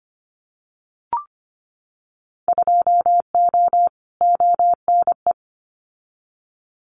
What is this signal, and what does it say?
Morse code, heard as a keyed tone, about 25 wpm, 700 Hz. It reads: 2O ONE
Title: Winter Field Day Exchanges Encore 25wpm